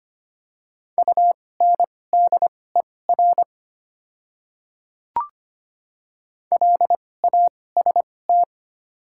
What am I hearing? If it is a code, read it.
Morse code, heard as a keyed tone, about 25 wpm, spaced as for 18 wpm, 700 Hz. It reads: UNDER LAST
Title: 2x Character Spacing - Top 300 Words - 25wpm